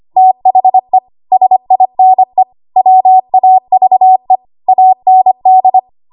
Morse code, heard as a keyed tone, about 25 wpm, 750 Hz. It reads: THE SINE WAVE AND
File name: Archive / 25 WPM / 25WPM